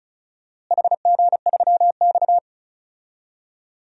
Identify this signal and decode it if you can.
Morse code, heard as a keyed tone, about 34 wpm, 700 Hz. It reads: HG3X
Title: DXCC-100 Call Signs - 4 Characters - Repeated 2x - 35wpm